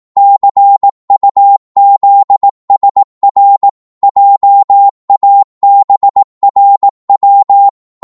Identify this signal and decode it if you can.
Morse code, heard as a keyed tone, about 18 wpm, 800 Hz. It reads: CUZSR JABRW